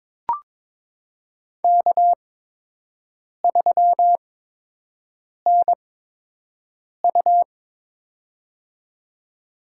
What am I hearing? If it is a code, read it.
Morse code, heard as a keyed tone, about 22 wpm, spaced as for 6 wpm, 700 Hz. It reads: K3NU
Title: USA Call Signs - 8x Character Spacing - 22wpm